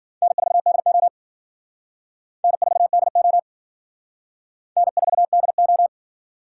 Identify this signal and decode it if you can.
Morse code, heard as a keyed tone, about 50 wpm, 700 Hz. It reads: N4DO N4DO N4DO
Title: Speed Racing - USA 1x2 Call Signs 40wpm